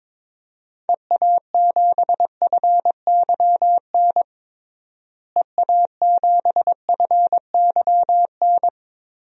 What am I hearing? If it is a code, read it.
Morse code, heard as a keyed tone, about 22 wpm, 700 Hz. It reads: EA7FYN EA7FYN
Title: DXCC-100 Call Signs - 6 Characters - Repeated 2x - 22wpm